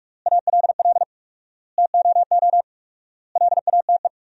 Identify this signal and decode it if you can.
Morse code, heard as a keyed tone, about 46 wpm, 700 Hz. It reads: ALL TOO LATE
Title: 4x Word-Spacing - Sentences from Top 300 Words - 45wpm